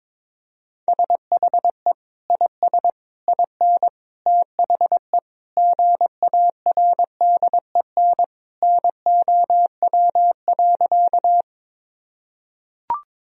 Morse code, heard as a keyed tone, about 22 wpm, 700 Hz. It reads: SHE IS IN THE GARDEN NOW.
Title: Sentences from Top 1000 Words - Review Part 4 - 22wpm